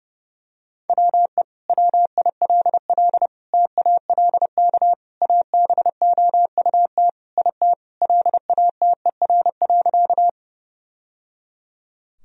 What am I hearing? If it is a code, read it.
Morse code, heard as a keyed tone, about 30 wpm, 700 Hz. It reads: WE WILL TALK ABOUT IT LATER.
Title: Sentences from Top 1000 Words - Review Part 4 - 30wpm